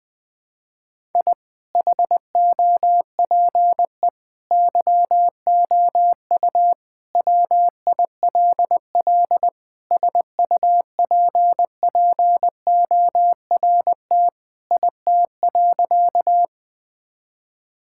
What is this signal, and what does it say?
Morse code, heard as a keyed tone, about 20 wpm, 700 Hz. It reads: I HOPE YOU WILL SUPPORT IT.
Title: Sentences from Top 1000 Words - Review Part 5 - 20wpm